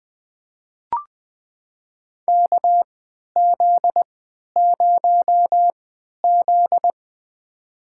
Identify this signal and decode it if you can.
Morse code, heard as a keyed tone, about 20 wpm, spaced as for 11 wpm, 700 Hz. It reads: KZ0Z
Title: USA Call Signs - 4 Characters - 3x Character Spacing - 20wpm